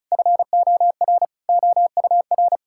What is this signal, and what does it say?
Morse code, heard as a keyed tone, about 34 wpm, 700 Hz. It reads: FOR OUR
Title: Sentences from Top 1000 Words - Review Part 5 - 35wpm